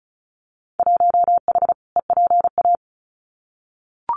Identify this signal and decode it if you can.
Morse code, heard as a keyed tone, about 36 wpm, 700 Hz. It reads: 1H EPA
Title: Winter Field Day Exchanges 35wpm